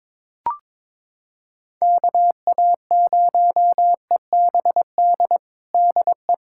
Morse code, heard as a keyed tone, about 22 wpm, 700 Hz. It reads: KA0EBD DE
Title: USA Callsign DE Callsign - 6 Characters - 22wpm